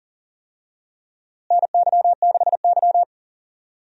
Morse code, heard as a keyed tone, about 40 wpm, 700 Hz. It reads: NY6Y
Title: USA Call Signs - 4 Characters - Repeated 3x - 40wpm